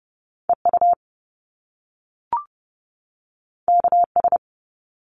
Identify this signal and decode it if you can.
Morse code, heard as a keyed tone, about 30 wpm, 700 Hz. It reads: EU KS
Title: Top Two Letter Combinations in Words 30wpm